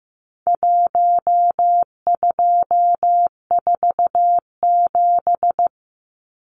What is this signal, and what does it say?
Morse code, heard as a keyed tone, about 15 wpm, 700 Hz. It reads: 1247
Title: Numbers - Contest Style - 4-Digit 15wpm